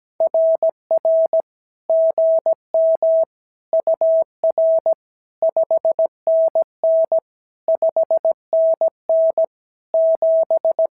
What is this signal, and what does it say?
Morse code, heard as a keyed tone, about 17 wpm, 650 Hz. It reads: RR GM UR 5NN 5NN 7
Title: SOTA QSO Practice 17wpm